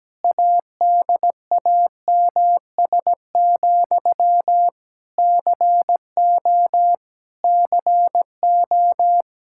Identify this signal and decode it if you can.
Morse code, heard as a keyed tone, about 17 wpm, 700 Hz. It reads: ADAMS, CO CO
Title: Practice QSO - Exchange 1 - Beginning - 17wpm